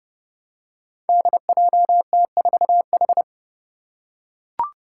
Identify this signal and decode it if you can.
Morse code, heard as a keyed tone, about 30 wpm, 700 Hz. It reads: DJT4H
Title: Five Challenging Letter-Number Combinations 30wpm